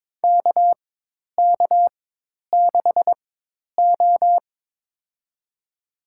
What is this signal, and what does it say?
Morse code, heard as a keyed tone, about 22 wpm, spaced as for 10 wpm, 700 Hz. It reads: KK6O
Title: USA Call Signs - 4 Characters - 4x Character Spacing - 22wpm